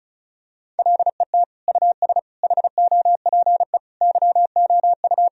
Morse code, heard as a keyed tone, about 36 wpm, 700 Hz. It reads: LET US HOPE YOU
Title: Sentences from Top 1000 Words - Review Part 1 - 35wpm